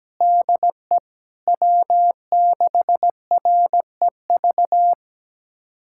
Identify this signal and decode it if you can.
Morse code, heard as a keyed tone, about 17 wpm, 700 Hz. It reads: DE W6REV
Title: USA Callsign DE Callsign 17wpm